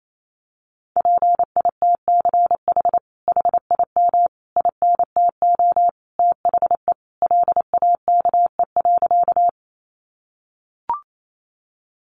Morse code, heard as a keyed tone, about 28 wpm, 700 Hz. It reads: PITCH HIM INTO THE LAKE.
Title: Sentences from Top 1300 Words 28wpm